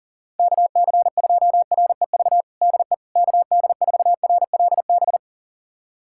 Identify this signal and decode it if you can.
Morse code, heard as a keyed tone, about 37 wpm, 700 Hz. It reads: KC2REV DE KD4RLB
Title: USA Callsign DE Callsign - 6 Characters - 40wpm